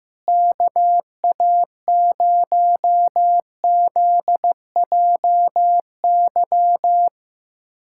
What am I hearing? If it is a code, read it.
Morse code, heard as a keyed tone, about 15 wpm, 700 Hz. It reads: KA0ZJY
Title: USA Callsign DE Callsign 15wpm